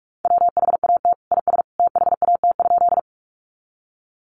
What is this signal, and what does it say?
Morse code, heard as a keyed tone, about 44 wpm, 700 Hz. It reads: WHAT IS THAT?
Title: Sentences from Top 100 Words 45wpm